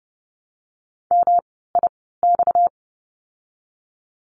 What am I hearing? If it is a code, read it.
Morse code, heard as a keyed tone, about 30 wpm, spaced as for 17 wpm, 700 Hz. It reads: MIX
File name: course / 30:10wpm / 230 Lesson 37f - Letter x - Words 30wpm